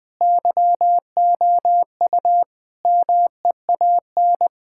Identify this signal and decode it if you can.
Morse code, heard as a keyed tone, about 20 wpm, 700 Hz. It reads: YOU MEAN
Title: Sentences with 5-Word Phrases - 20wpm